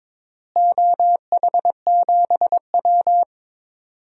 Morse code, heard as a keyed tone, about 22 wpm, 700 Hz. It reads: OH7W
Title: DXCC-100 Call Signs - 4 Characters - Repeated 2x - 22wpm